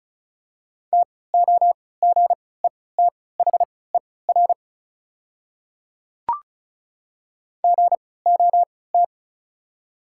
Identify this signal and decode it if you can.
Morse code, heard as a keyed tone, about 36 wpm, spaced as for 20 wpm, 700 Hz. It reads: TOGETHER GOT
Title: 3x Character Spacing - Top 300 Words - 35wpm